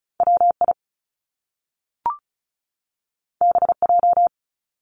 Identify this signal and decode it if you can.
Morse code, heard as a keyed tone, about 36 wpm, 700 Hz. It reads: WI BJ
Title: Top Two Letter Combinations in Words 35wpm